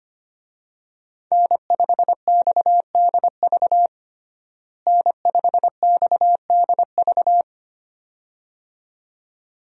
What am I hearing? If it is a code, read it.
Morse code, heard as a keyed tone, about 25 wpm, 700 Hz. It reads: N5XDV N5XDV
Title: USA Call Signs - 5 Characters - Repeated 2x - 25wpm